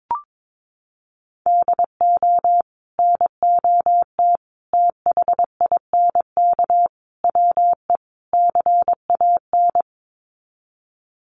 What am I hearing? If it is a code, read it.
Morse code, heard as a keyed tone, about 22 wpm, 700 Hz. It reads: DO NOT THINK WE CAN
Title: Sets of 5 Words Encore2 22wpm